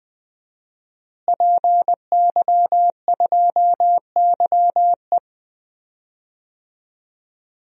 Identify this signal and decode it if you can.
Morse code, heard as a keyed tone, about 20 wpm, 700 Hz. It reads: PY2YE
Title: DXCC-100 Call Signs - 5 Characters - Repeated 2x - 20wpm